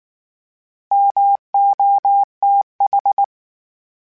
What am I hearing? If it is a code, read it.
Morse code, heard as a keyed tone, about 19 wpm, 800 Hz. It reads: MOTH